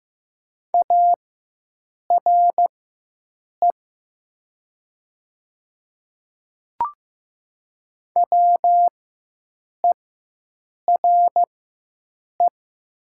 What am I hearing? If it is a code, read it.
Morse code, heard as a keyed tone, about 15 wpm, spaced as for 7 wpm, 700 Hz. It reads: ARE WERE
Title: 4x Character Spacing - Top 100 Words - 15wpm